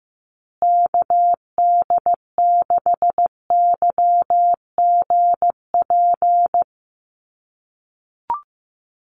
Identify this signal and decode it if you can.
Morse code, heard as a keyed tone, about 15 wpm, 700 Hz. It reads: KD6YGP
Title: USA Call Signs Repeated 3x 15wpm